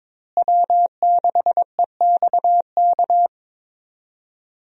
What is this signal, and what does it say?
Morse code, heard as a keyed tone, about 22 wpm, 700 Hz. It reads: W6EXK
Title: USA Call Signs - 5 Characters - Repeated 2x - 22wpm